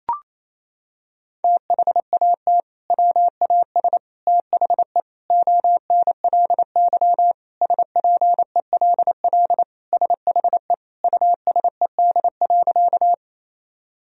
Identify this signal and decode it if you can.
Morse code, heard as a keyed tone, about 28 wpm, 700 Hz. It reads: THAT WAS THE ONLY SPELL SHE USED.
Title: Sentences from Top 1000 Words - Review Part 1 - 28wpm